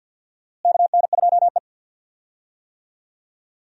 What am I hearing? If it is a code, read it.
Morse code, heard as a keyed tone, about 47 wpm, 700 Hz. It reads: KN2E
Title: USA Call Signs - 4 Characters - Repeated 3x - 50wpm